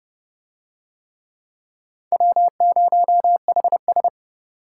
Morse code, heard as a keyed tone, about 29 wpm, 700 Hz. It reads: W0HS